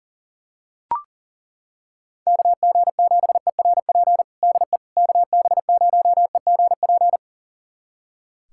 Morse code, heard as a keyed tone, about 40 wpm, 700 Hz. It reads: KG7ERP DE KB0EGP
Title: USA Callsign DE Callsign 40wpm